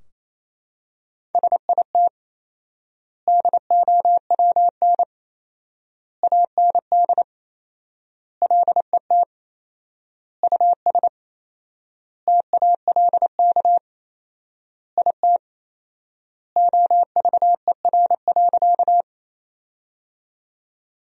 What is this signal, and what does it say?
Morse code, heard as a keyed tone, about 28 wpm, 700 Hz. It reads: SIT DOWN AND LET US TALK IT OVER.
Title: 4x Word-Spacing - Sentences from Top 500 Words - 28wpm